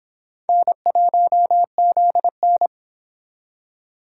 Morse code, heard as a keyed tone, about 26 wpm, 700 Hz. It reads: N1ZN